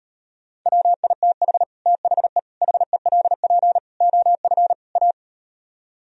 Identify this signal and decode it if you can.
Morse code, heard as a keyed tone, about 38 wpm, 700 Hz. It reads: WITH THE HELP OF A